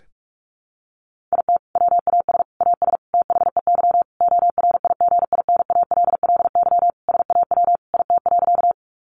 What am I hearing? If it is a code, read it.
Morse code, heard as a keyed tone, about 44 wpm, 700 Hz. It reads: IT WAS AS THEY ORIGINALLY SAW IT.
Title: Sentences from Top 1500 Words 45wpm